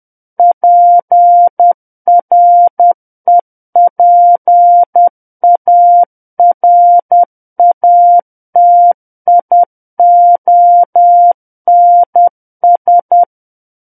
Morse code, heard as a keyed tone, about 10 wpm, 700 Hz. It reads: PREPARATIONS
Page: War of the Worlds - 01-Chapter 1 - 10 WPM